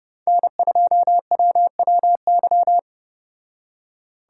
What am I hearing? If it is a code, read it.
Morse code, heard as a keyed tone, about 32 wpm, 700 Hz. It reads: N2WWY